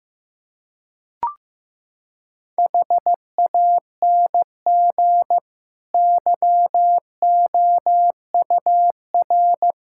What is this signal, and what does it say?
Morse code, heard as a keyed tone, about 15 wpm, 700 Hz. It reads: HANG YOUR
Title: Sentences from Top 1600 Words 15wpm